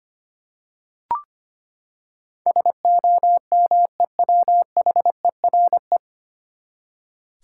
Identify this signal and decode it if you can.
Morse code, heard as a keyed tone, about 25 wpm, 700 Hz. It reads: SOMEWHERE
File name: Nine Letter Words - 25wpm